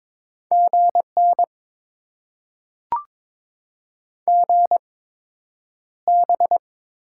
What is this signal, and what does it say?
Morse code, heard as a keyed tone, about 22 wpm, spaced as for 9 wpm, 700 Hz. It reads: GN GB
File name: Top 90 QSO Elements - 8x Character Spacing - 22wpm